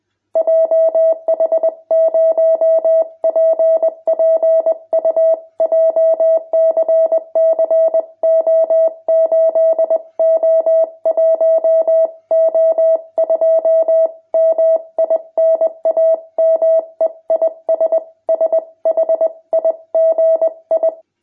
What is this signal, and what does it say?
Morse code, heard as a keyed tone, about 20 wpm, 650 Hz. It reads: JH0PPU JCC O8O1O2 MI NAME IS SHIGI
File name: jh8cbh / 2021cw